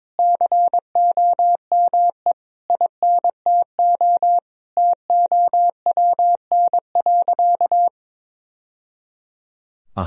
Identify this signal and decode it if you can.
Morse code, heard as a keyed tone, about 22 wpm, 700 Hz. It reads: COME INTO TOWN.
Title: Sentences from Top 1000 Words - Review Part 2 - 22wpm